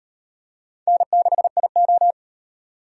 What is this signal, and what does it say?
Morse code, heard as a keyed tone, about 36 wpm, 700 Hz. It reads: N6IO